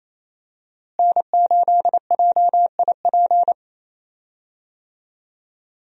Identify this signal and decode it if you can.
Morse code, heard as a keyed tone, about 28 wpm, 700 Hz. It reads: N8JIP
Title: USA Call Signs - 5 Characters - Repeated 2x - 28wpm